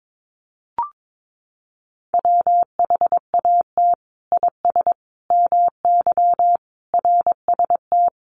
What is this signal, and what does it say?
Morse code, heard as a keyed tone, about 22 wpm, 700 Hz. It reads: WHAT IS MY RST
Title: Common QSO Phrases 22wpm